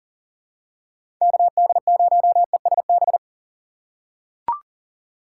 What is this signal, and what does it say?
Morse code, heard as a keyed tone, about 40 wpm, 700 Hz. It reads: KD0ESB
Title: USA Call Signs - 2x Character Spacing - 40wpm